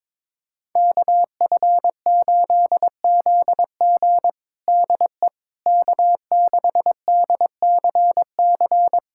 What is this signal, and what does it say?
Morse code, heard as a keyed tone, about 22 wpm, 700 Hz. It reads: KF8ZG DE K6DCC
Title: USA Callsign DE Callsign - 5 Characters - 22wpm